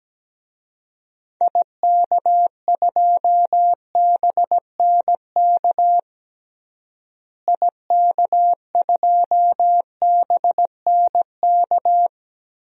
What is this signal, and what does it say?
Morse code, heard as a keyed tone, about 17 wpm, 700 Hz. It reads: IK2BNK IK2BNK
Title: DXCC-100 Call Signs - 6 Characters - Repeated 2x - 17wpm